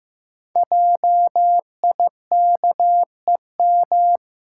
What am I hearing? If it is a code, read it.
Morse code, heard as a keyed tone, about 15 wpm, 700 Hz. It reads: JIKEM
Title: Code Groups - 5 Letters - 15wpm